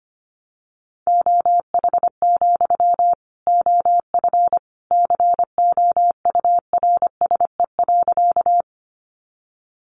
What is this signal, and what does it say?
Morse code, heard as a keyed tone, about 25 wpm, 700 Hz. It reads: OH, OF COURSE.
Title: Sentences from Top 500 Words 25wpm